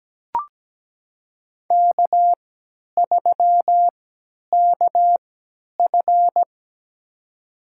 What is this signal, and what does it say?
Morse code, heard as a keyed tone, about 17 wpm, spaced as for 10 wpm, 700 Hz. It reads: K3KF
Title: USA Call Signs - 3x Character Spacing - 17wpm